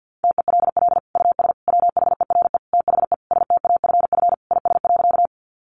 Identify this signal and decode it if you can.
Morse code, heard as a keyed tone, about 50 wpm, 700 Hz. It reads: TELL US WHERE THE STAFF IS.
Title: Sentences from Top 1200 Words 50wpm